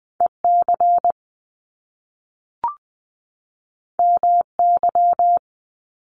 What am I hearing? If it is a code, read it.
Morse code, heard as a keyed tone, about 20 wpm, 700 Hz. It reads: E C M Y
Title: Top Two Letter Combinations in Words 20wpm